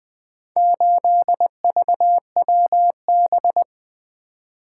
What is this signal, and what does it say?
Morse code, heard as a keyed tone, about 20 wpm, 700 Hz. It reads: 8VWB
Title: Four Letter-Number Combinations - 20wpm